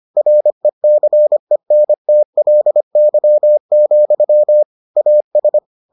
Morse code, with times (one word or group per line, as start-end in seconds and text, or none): 0.2-4.6 RECENTLY,
5.0-5.6 AS